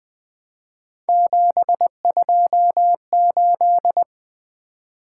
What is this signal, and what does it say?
Morse code, heard as a keyed tone, about 25 wpm, 700 Hz. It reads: 728
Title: Speed-Racing - Three Letter-Number Combinations - 20wpm